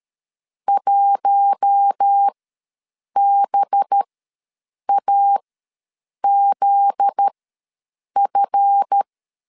Morse code, heard as a keyed tone, about 13 wpm, spaced as for 7 wpm, 800 Hz. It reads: 1BAZF
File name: LEZIONI / 7 GF9Z / 7 GF9Z ws 12wpm 3321